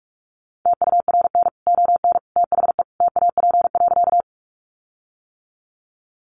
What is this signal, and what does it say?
Morse code, heard as a keyed tone, about 44 wpm, 700 Hz. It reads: TURN ON THE TAP.
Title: Sentences from Top 1400 Words 45wpm